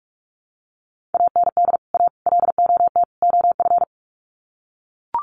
Morse code, heard as a keyed tone, about 44 wpm, 700 Hz. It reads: AND A LOT OF
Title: Sets of 4 Words 45wpm